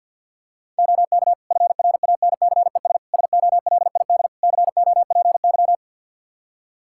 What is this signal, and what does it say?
Morse code, heard as a keyed tone, about 51 wpm, 700 Hz. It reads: OK FRANCES SOLID COPY
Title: Practice QSO - Exchange 3 - More Info - 50wpm